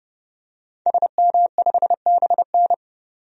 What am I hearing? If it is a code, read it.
Morse code, heard as a keyed tone, about 30 wpm, 700 Hz. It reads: SM5BN
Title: DXCC-100 Call Signs - 5 Characters - 30wpm